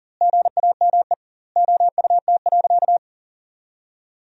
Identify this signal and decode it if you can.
Morse code, heard as a keyed tone, about 37 wpm, 700 Hz. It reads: GAME OUT.
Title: 2x Word-Spacing - Sentences from Top 500 Words - 40wpm